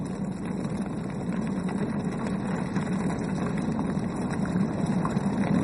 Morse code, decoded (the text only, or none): none